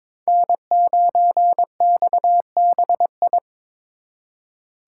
Morse code, heard as a keyed tone, about 26 wpm, 700 Hz. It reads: N9XBI
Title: Speed Racing - USA 1x3 Call Signs 22wpm